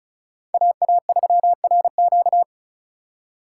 Morse code, heard as a keyed tone, about 36 wpm, 700 Hz. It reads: AA3RQ
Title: USA Call Signs - 5 Characters - 3x Character Spacing - 35wpm